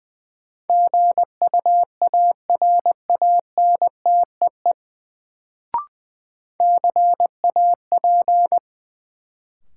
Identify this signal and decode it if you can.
Morse code, heard as a keyed tone, about 20 wpm, 700 Hz. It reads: GUARANTEE CAP
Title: Sentences from Top 1600 Words - New Word Drill Alt - 20wpm